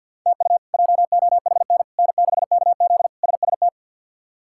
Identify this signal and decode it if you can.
Morse code, heard as a keyed tone, about 53 wpm, 700 Hz. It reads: TU JOHN N6KZ SST
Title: K1USN SST QSO Practice - 50wpm